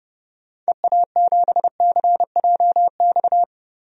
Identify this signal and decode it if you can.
Morse code, heard as a keyed tone, about 30 wpm, 700 Hz. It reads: EA7CJX
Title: DXCC-100 Call Signs - 6 Characters - Repeated 3x - 30wpm